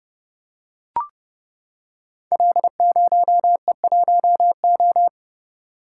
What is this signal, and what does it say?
Morse code, heard as a keyed tone, about 28 wpm, 700 Hz. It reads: L0E1O
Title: Code Groups - 5 Letter:Numbers - 30wpm